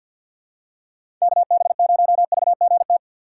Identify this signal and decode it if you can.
Morse code, heard as a keyed tone, about 53 wpm, 700 Hz. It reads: KB0VGT
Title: USA Call Signs - 4x Character Spacing - 50wpm